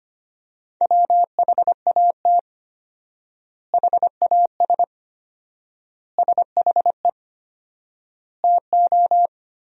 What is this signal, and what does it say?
Morse code, heard as a keyed tone, about 25 wpm, 700 Hz. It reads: WHAT HAS SHE TO
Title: 4x Word-Spacing - Sentences from Top 100 Words - 25wpm